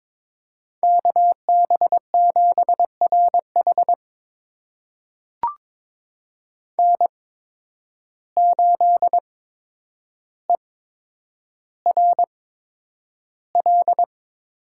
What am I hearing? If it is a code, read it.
Morse code, heard as a keyed tone, about 22 wpm, spaced as for 9 wpm, 700 Hz. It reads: KB7RH N8ERL
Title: USA Call Signs - 8x Character Spacing - 22wpm